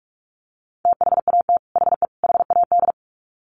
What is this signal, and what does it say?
Morse code, heard as a keyed tone, about 44 wpm, 700 Hz. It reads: THAT HE HAD
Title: Sets of 3 Words 45wpm